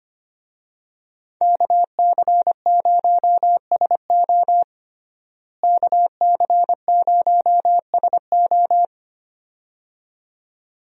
Morse code, heard as a keyed tone, about 25 wpm, 700 Hz. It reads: KC0SO KC0SO
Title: USA Call Signs - 5 Characters - Repeated 2x - 25wpm